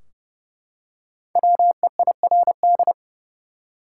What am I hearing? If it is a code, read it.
Morse code, heard as a keyed tone, about 29 wpm, 700 Hz. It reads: WEIRD
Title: Five Letter Words - 30wpm